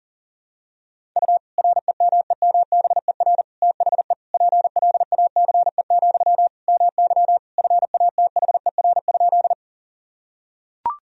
Morse code, heard as a keyed tone, about 40 wpm, 700 Hz. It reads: U REMEMBER THE PLACE, MY FATHER?
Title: Sentences from Top 400 Words and Abbreviations 40wpm